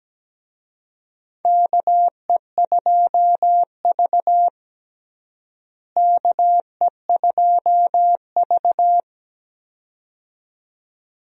Morse code, heard as a keyed tone, about 17 wpm, 700 Hz. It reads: KE2V KE2V
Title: USA Call Signs - 4 Characters - Repeated 2x - 17wpm